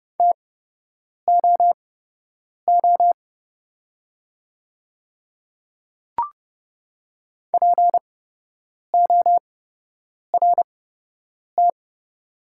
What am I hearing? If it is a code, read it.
Morse code, heard as a keyed tone, about 30 wpm, spaced as for 8 wpm, 690 Hz. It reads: TOO PORT
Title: 8x Character Spacing - Top 200 Words - 30wpm